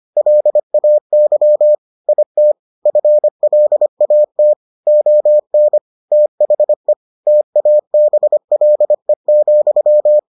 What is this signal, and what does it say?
Morse code, heard as a keyed tone, about 25 wpm, 600 Hz. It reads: LAY IT FLAT ON THE TABLE,